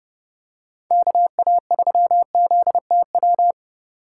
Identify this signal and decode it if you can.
Morse code, heard as a keyed tone, about 30 wpm, 700 Hz. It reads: KA3ZTW